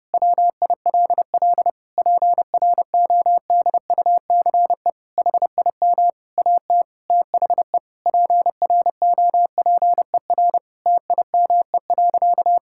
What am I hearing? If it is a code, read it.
Morse code, heard as a keyed tone, about 30 wpm, 700 Hz. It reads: WILL PRODUCE HIM AT THE PROPER TIME.
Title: Sentences from Top 1000 Words - Review Part 5 - 30wpm